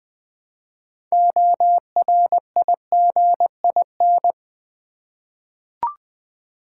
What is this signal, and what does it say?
Morse code, heard as a keyed tone, about 20 wpm, 700 Hz. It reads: ORIGIN
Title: 6 Letter Words - First 3 Letters Given - 20wpm